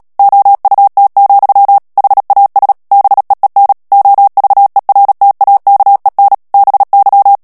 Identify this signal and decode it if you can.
Morse code, heard as a keyed tone, about 37 wpm, 800 Hz. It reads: OUT, HAS BEEN OVERTAKEN BY
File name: morse mp3 / ANGLAIS texte clair 3 vitesse 35 mots minute